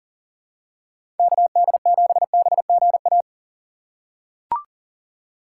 Morse code, heard as a keyed tone, about 40 wpm, 700 Hz. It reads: KD7BGA
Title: USA Call Signs - 6 Characters - 3x Character Spacing - 40wpm